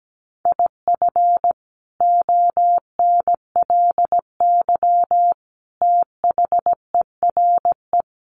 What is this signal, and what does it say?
Morse code, heard as a keyed tone, about 17 wpm, 700 Hz. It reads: IF ONLY THERE
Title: Sentences from Top 1400 Words 17wpm